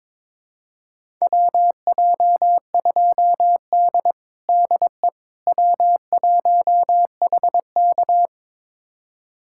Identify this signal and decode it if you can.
Morse code, heard as a keyed tone, about 22 wpm, 700 Hz. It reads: WJ2D DE W1HK
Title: USA Callsign DE Callsign - 4 Characters - 22wpm